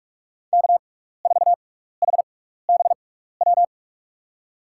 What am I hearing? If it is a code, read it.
Morse code, heard as a keyed tone, about 44 wpm, spaced as for 15 wpm, 700 Hz. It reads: K4HBW
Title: USA Call Signs - 5 Characters - 6x Character Spacing - 45wpm